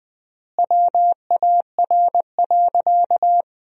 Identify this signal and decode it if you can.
Morse code, heard as a keyed tone, about 19 wpm, 700 Hz. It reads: WAR.
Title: 3x Word-Spacing - Sentences from Top 400 Words - 20wpm